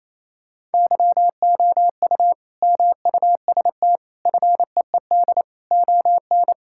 Morse code, heard as a keyed tone, about 28 wpm, 700 Hz. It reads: YOU MUST FEED ON
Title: Sentences from Top 1600 Words 28wpm